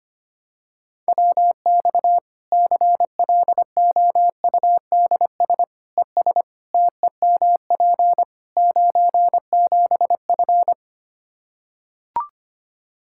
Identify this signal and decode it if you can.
Morse code, heard as a keyed tone, about 25 wpm, 700 Hz. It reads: WX CLOUDS ES TEMP 97F
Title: Practice QSO - Exchange 2 - Info - 25wpm